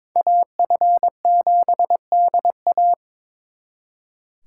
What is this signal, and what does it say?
Morse code, heard as a keyed tone, about 26 wpm, 700 Hz. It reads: AF7DA
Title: Speed Racing - USA Call Signs 22wpm